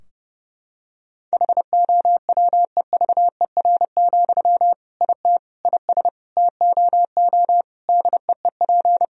Morse code, heard as a keyed tone, about 30 wpm, 700 Hz. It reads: HOWEVER, IT IS TOO DEEP
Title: Sentences from Top 1000 Words - Review Part 3 - 30wpm